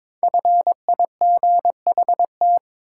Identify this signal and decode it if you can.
Morse code, heard as a keyed tone, about 22 wpm, 700 Hz. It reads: FIGHT
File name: 5 Letter Words - First 3 Letters Given - 22wpm